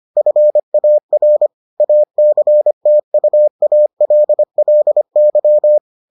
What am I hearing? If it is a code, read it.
Morse code, heard as a keyed tone, about 25 wpm, 600 Hz. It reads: FAR ACTUALLY